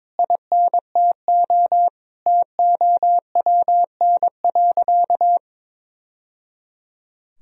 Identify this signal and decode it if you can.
Morse code, heard as a keyed tone, about 22 wpm, 700 Hz. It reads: INTO TOWN.
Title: Sentences from Top 1000 Words - Review Part 2 - 22wpm